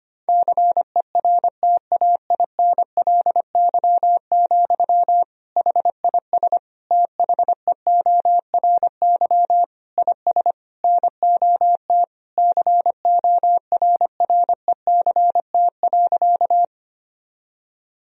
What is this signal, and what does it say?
Morse code, heard as a keyed tone, about 25 wpm, 700 Hz. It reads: CERTAINLY, HIS THEORY IS NOT CORRECT.
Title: Sentences from Top 1000 Words - Review Part 3 - 25wpm